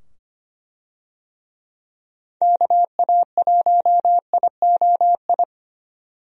Morse code, heard as a keyed tone, about 25 wpm, 700 Hz. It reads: KA1IOI
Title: USA Call Signs - 6 Characters - 25wpm